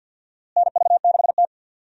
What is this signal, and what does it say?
Morse code, heard as a keyed tone, about 49 wpm, 700 Hz. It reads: NV6T
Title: USA Call Signs - 4 Characters - Repeated 2x - 50wpm